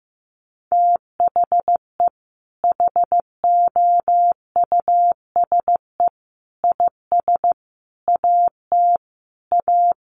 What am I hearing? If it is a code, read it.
Morse code, heard as a keyed tone, about 15 wpm, 700 Hz. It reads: THE HOUSE IS AT A
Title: Sentences from Top 1500 Words 15wpm